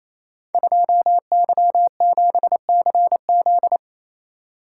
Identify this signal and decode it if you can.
Morse code, heard as a keyed tone, about 28 wpm, 700 Hz. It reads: 2Y7CZ
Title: Code Groups - 5 Letter:Numbers - 28wpm